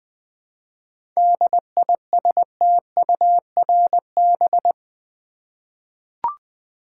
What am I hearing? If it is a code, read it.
Morse code, heard as a keyed tone, about 20 wpm, 700 Hz. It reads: DISTURB
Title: 7 Letter Words - First 3 Letters Given - 20wpm